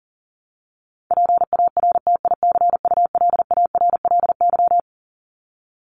Morse code, heard as a keyed tone, about 40 wpm, 700 Hz. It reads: PARTICULARLY
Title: LY Suffix - Top 100 Words with Base Words 40wpm